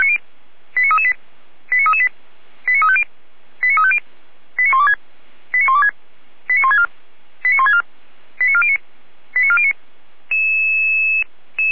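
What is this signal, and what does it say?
Five-tone selective call (ZVEI), heard as quick runs of five tones, tones of about 70 ms. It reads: 89338 89338 89350 89350 89126 89126 89164 89164 89409 89409
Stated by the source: ZVEI five-tone selective call received on an SDR.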